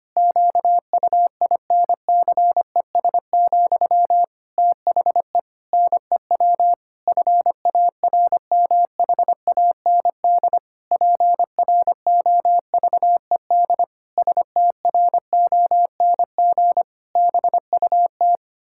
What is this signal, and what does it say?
Morse code, heard as a keyed tone, about 25 wpm, 700 Hz. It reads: QUINCES, THE NEW FARMHAND PROVED STRONG BUT
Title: Pangram Challenge v2 - 25wpm